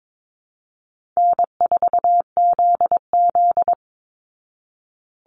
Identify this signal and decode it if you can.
Morse code, heard as a keyed tone, about 22 wpm, 700 Hz. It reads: N4ZZ
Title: Call Signs Repeated 3x - 22wpm